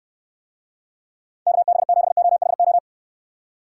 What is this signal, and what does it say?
Morse code, heard as a keyed tone, about 69 wpm, 700 Hz. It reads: KB7OHO